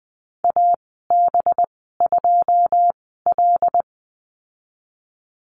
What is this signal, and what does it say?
Morse code, heard as a keyed tone, about 20 wpm, spaced as for 14 wpm, 700 Hz. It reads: AB2L
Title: 213 Lesson 34g - Number 8 - Call Signs 20wpm